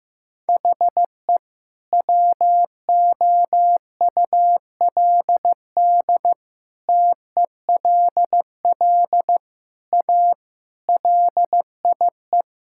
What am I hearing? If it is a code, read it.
Morse code, heard as a keyed tone, about 15 wpm, 700 Hz. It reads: HE WOULD TELL A LIE
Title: Sentences from Top 1600 Words 15wpm